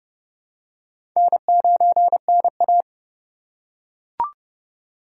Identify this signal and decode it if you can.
Morse code, heard as a keyed tone, about 30 wpm, 700 Hz. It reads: N9NA
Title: USA Call Signs - 4 Characters - 3x Character Spacing - 30wpm